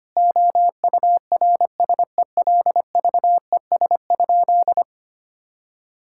Morse code, heard as a keyed tone, about 25 wpm, 700 Hz. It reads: OURSELVES?
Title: Sentences from Top 1600 Words 25wpm